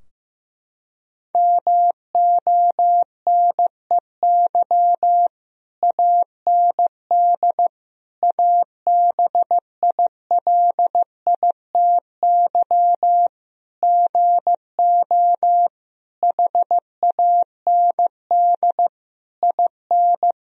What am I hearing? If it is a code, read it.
Morse code, heard as a keyed tone, about 15 wpm, 700 Hz. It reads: MONEY AND ABILITY GO HAND IN
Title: Sentences from Top 1000 Words - Review Part 3 - 15wpm